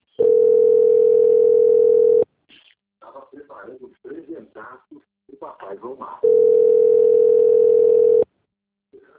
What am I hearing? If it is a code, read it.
Ringback tone (North America), 440 + 480 Hz, with speech.
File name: out- - - - - .6977